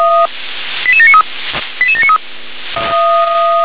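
Five-tone selective call (ZVEI), heard as quick runs of five tones, tones of about 80 ms. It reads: 88793 88793
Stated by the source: Five-tone ZVEI selective call received by SDR.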